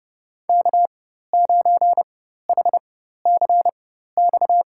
Morse code, heard as a keyed tone, about 32 wpm, spaced as for 14 wpm, 700 Hz. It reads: K9HCX
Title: USA Call Signs - 5 Characters - 4x Character Spacing - 30wpm